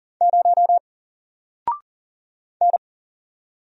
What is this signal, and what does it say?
Morse code, heard as a keyed tone, about 40 wpm, 700 Hz. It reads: 0 N
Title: Single Letter-Number 40wpm